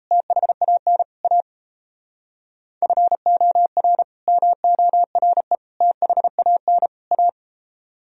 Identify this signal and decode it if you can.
Morse code, heard as a keyed tone, about 34 wpm, 700 Hz. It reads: THAN A FOR MORE THAN A
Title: Speed-Racing - Sets of 4 Words 28wpm